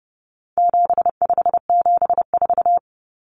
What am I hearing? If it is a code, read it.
Morse code, heard as a keyed tone, about 30 wpm, 700 Hz. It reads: 7574
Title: Common Cut Numbers - 4-Digit 30wpm